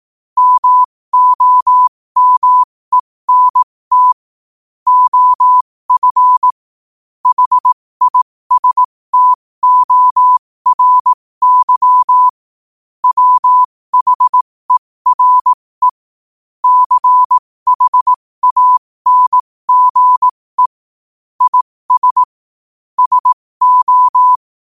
Morse code, heard as a keyed tone, about 18 wpm, spaced as for 15 wpm, 1000 Hz. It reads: MOMENT OF HISTORY WHERE CHANGE IS SO
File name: qotdfiles / 15wpm